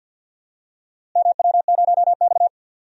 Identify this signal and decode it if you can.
Morse code, heard as a keyed tone, about 53 wpm, 700 Hz. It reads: MW0X
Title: DXCC-100 Call Signs - 4 Characters - Repeated 2x - 50wpm